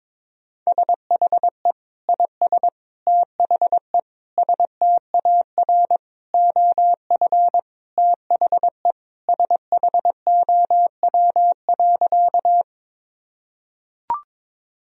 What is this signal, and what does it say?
Morse code, heard as a keyed tone, about 22 wpm, 700 Hz. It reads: SHE IS THE STAR OF THE SHOW.
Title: Sentences from Top 1000 Words - Review Part 2 - 22wpm